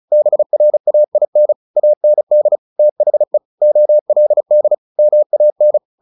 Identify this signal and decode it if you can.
Morse code, heard as a keyed tone, about 34 wpm, 600 Hz. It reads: BRAIN AND THE OLD MAN
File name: the old man and the sea 2 35wpm